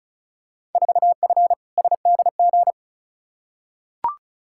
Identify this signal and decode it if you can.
Morse code, heard as a keyed tone, about 36 wpm, 700 Hz. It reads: 4F SDG
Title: Field Day Exchanges - 35wpm